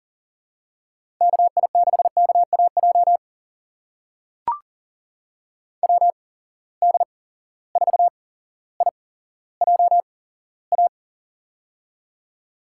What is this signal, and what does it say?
Morse code, heard as a keyed tone, about 39 wpm, spaced as for 17 wpm, 700 Hz. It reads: KI6KAJ WD4IJA
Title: USA Call Signs - 6 Characters - 8x Character Spacing - 40wpm